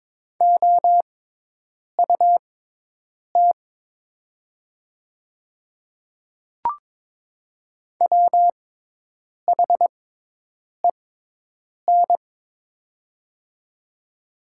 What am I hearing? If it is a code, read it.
Morse code, heard as a keyed tone, about 22 wpm, spaced as for 8 wpm, 700 Hz. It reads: OUT WHEN
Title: 6x Character Spacing - Top 200 Words - 22wpm